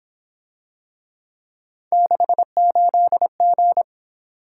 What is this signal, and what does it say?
Morse code, heard as a keyed tone, about 26 wpm, 700 Hz. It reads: 68G
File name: Speed-Racing - Three Letter-Number Combinations - 22wpm